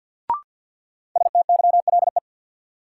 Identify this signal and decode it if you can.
Morse code, heard as a keyed tone, about 48 wpm, 700 Hz. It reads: STYLE
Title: Sentences from Top 800 Words - New Word Drill - 50wpm